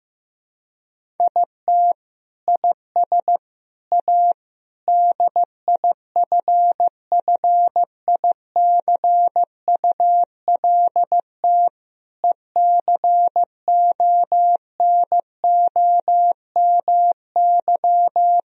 Morse code, heard as a keyed tone, about 15 wpm, 700 Hz. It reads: IT IS A DIFFICULT ECONOMY
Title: Sentences from Top 1000 Words - Review Part 3 - 15wpm